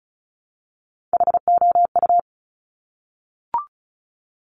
Morse code, heard as a keyed tone, about 36 wpm, 700 Hz. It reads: HOU
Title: Top Three Letter Combinations in Words 35wpm